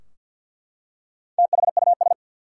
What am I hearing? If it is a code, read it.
Morse code, heard as a keyed tone, about 50 wpm, 700 Hz. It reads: THUS
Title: Four Letter Words - 50wpm